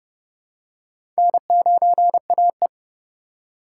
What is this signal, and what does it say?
Morse code, heard as a keyed tone, about 30 wpm, 700 Hz. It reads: N9AE
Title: USA Call Signs - 4 Characters - 4x Character Spacing - 30wpm